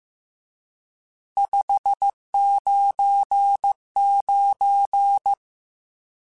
Morse code, heard as a keyed tone, about 15 wpm, 800 Hz. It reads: 599
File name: RSTs / RST-mp3s / Various RST @13